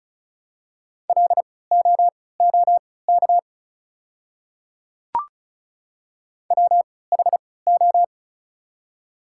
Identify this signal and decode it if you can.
Morse code, heard as a keyed tone, about 36 wpm, spaced as for 20 wpm, 700 Hz. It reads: LOOK WHO
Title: 3x Character Spacing - Top 100 Words - 35wpm